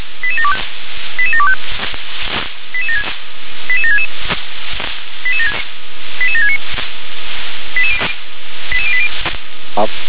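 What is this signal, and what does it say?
Five-tone selective call (ZVEI), heard as quick runs of five tones, tones of about 70 ms. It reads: 88725 88766 88769 88080
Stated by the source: ZVEI five-tone call received on SDR.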